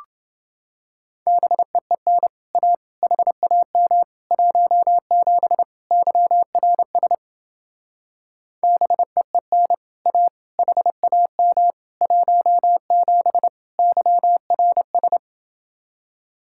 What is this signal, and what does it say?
Morse code, heard as a keyed tone, about 29 wpm, 700 Hz. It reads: BEEN A HAM 17 YRS BEEN A HAM 17 YRS